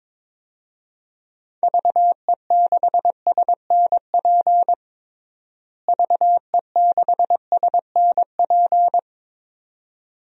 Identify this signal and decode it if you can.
Morse code, heard as a keyed tone, about 22 wpm, 700 Hz. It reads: VE6SNP VE6SNP
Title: DXCC-100 Call Signs - 6 Characters - Repeated 2x - 22wpm